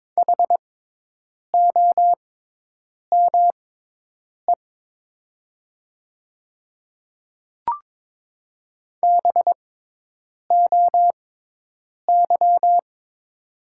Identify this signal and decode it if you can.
Morse code, heard as a keyed tone, about 22 wpm, spaced as for 8 wpm, 700 Hz. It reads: HOME BOY
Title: 6x Character Spacing - Top 200 Words - 22wpm